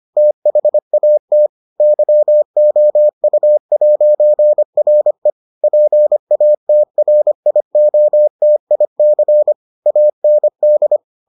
Morse code, heard as a keyed tone, about 25 wpm, 600 Hz. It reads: THAT YOU'RE PATRIOTIC AND